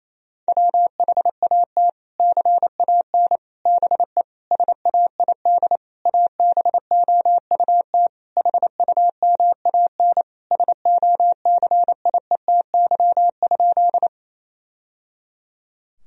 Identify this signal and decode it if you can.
Morse code, heard as a keyed tone, about 28 wpm, 700 Hz. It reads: WHAT CAN BE SAID ABOUT HUMAN SOCIETY?
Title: Sentences from Top 1000 Words - Review Part 4 - 28wpm